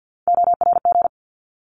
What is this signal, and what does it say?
Morse code, heard as a keyed tone, about 50 wpm, 700 Hz. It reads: ORG
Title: ARRL Sections 50wpm